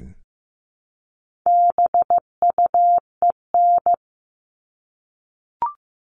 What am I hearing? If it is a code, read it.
Morse code, heard as a keyed tone, about 15 wpm, 700 Hz. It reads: BUEN
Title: All Cut Numbers - 4-Digit 15wpm